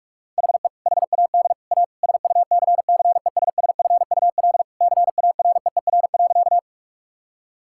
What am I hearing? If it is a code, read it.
Morse code, heard as a keyed tone, about 41 wpm, 700 Hz. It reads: HE HAD A SUCCESSFUL CAREER.
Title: Sentences from Top 1000 Words - Review Part 6 - 45wpm